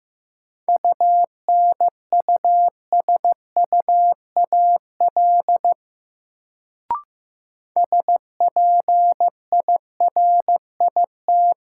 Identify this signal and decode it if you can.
Morse code, heard as a keyed tone, about 15 wpm, 700 Hz. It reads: UNUSUAL SPIRIT
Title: Sentences from Top 1400 Words - New Word Drill - 15wpm